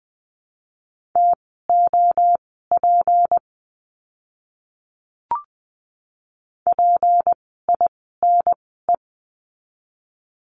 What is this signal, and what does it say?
Morse code, heard as a keyed tone, about 20 wpm, spaced as for 14 wpm, 700 Hz. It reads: TOP PINE
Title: 153 Lesson 25f - Letter p - Words 20wpm